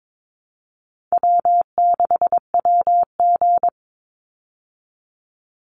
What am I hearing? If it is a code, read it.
Morse code, heard as a keyed tone, about 22 wpm, 700 Hz. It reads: W6WG
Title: Call Signs Repeated 3x - 22wpm